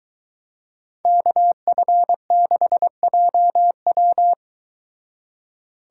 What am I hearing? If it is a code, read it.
Morse code, heard as a keyed tone, about 23 wpm, 700 Hz. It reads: KF6JW